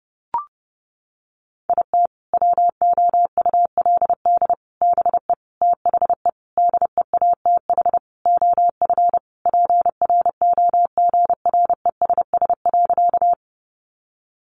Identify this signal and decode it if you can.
Morse code, heard as a keyed tone, about 30 wpm, 700 Hz. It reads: IT WOULD BE THE DEATH OF PROGRESS.
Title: Sentences from Top 1400 Words 30wpm